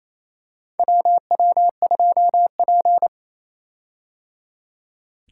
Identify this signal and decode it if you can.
Morse code, heard as a keyed tone, about 27 wpm, 700 Hz. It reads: WW2P
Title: K1USN SST QSO Practice - 28wpm